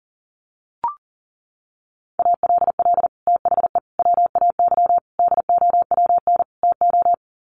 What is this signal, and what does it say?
Morse code, heard as a keyed tone, about 40 wpm, 700 Hz. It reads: ALL THE WAY DOWN TO
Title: Sets of 5 Words Encore2 40wpm